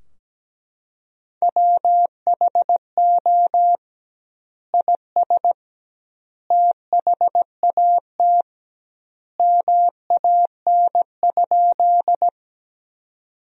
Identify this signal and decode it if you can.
Morse code, heard as a keyed tone, about 17 wpm, 700 Hz. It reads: WHO IS THAT MAN?
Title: 2x Word-Spacing - Sentences from Top 200 Words - 17wpm